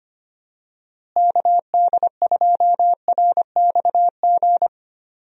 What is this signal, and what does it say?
Morse code, heard as a keyed tone, about 25 wpm, 700 Hz. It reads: KD2RXG
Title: USA Call Signs - 6 Characters - Repeated 2x - 25wpm